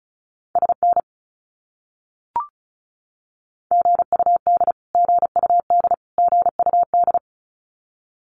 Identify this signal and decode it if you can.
Morse code, heard as a keyed tone, about 34 wpm, 700 Hz. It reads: SN GUD GUD GUD
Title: Abbreviations Repeated 3x - 35wpm